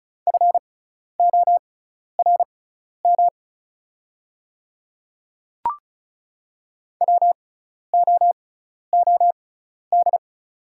Morse code, heard as a keyed tone, about 36 wpm, spaced as for 12 wpm, 690 Hz. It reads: FORM WOOD
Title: 6x Character Spacing - Top 500 Words - 35wpm